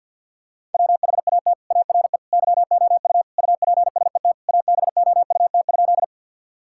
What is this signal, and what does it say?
Morse code, heard as a keyed tone, about 50 wpm, 700 Hz. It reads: WHAT ARE YOU UPSET ABOUT?
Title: Sentences from Top 1000 Words - Review Part 5 - 50wpm